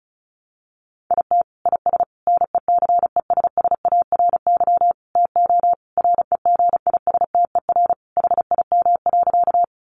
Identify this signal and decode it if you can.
Morse code, heard as a keyed tone, about 34 wpm, 700 Hz. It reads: IT IS NECESSARY TO REGISTER HIM.
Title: Sentences from Top 1200 Words 35wpm